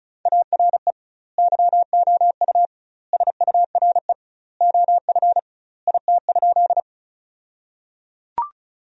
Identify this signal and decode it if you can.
Morse code, heard as a keyed tone, about 34 wpm, 700 Hz. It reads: ARE YOU SURE OF IT?
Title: 2x Word-Spacing - Sentences from Top 300 Words - 35wpm